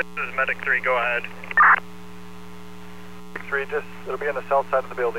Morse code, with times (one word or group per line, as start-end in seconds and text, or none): none